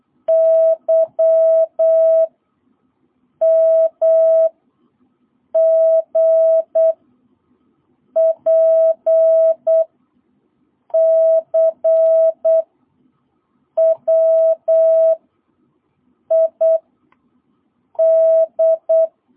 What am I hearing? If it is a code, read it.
Morse code, heard as a keyed tone, about 8 wpm, 650 Hz. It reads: Y M G P C W I D
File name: jh8cbh / 2022cw